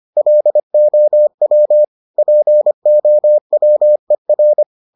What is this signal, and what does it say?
Morse code, heard as a keyed tone, about 25 wpm, 600 Hz. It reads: LOW POWER